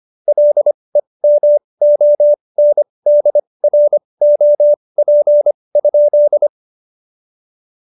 Morse code, heard as a keyed tone, about 25 wpm, spaced as for 20 wpm, 590 Hz. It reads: LEMONDROP?
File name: Jokes-25-char-20-eff